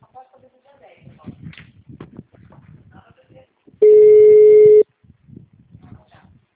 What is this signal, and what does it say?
Ringback tone (Europe, 425 Hz), with speech.
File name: out- -2011- - - .5699